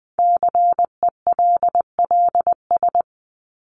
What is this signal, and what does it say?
Morse code, heard as a keyed tone, about 20 wpm, 700 Hz. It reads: CELLS
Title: ES:S Suffix - Top 100 Words with Base Words 20wpm